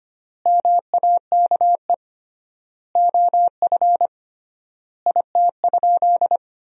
Morse code, heard as a keyed tone, about 25 wpm, 700 Hz. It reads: MAKE OF IT?
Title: 3x Word-Spacing - Sentences from Top 100 Words - 25wpm